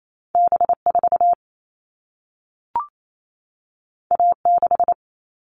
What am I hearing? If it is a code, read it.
Morse code, heard as a keyed tone, about 28 wpm, 700 Hz. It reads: B4 A6
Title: All Cut Numbers - 2-Digit 28wpm